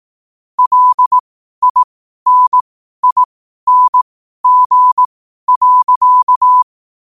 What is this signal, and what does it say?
Morse code, heard as a keyed tone, about 18 wpm, spaced as for 12 wpm, 1000 Hz. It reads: LINING.
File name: qotdfiles / 12wpm